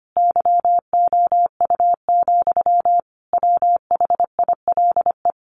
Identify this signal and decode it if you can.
Morse code, heard as a keyed tone, about 25 wpm, 700 Hz. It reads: YOU, WHILE
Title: Sentences from Top 1400 Words 25wpm